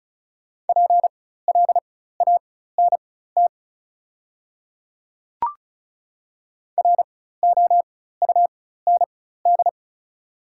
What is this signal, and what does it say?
Morse code, heard as a keyed tone, about 36 wpm, spaced as for 16 wpm, 690 Hz. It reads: PLANT ROUND
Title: 4x Character Spacing - Top 300 Words - 35wpm